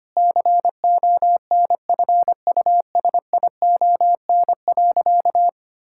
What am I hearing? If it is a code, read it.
Morse code, heard as a keyed tone, about 25 wpm, 700 Hz. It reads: CONFUSION.
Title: Sentences from Top 1600 Words 25wpm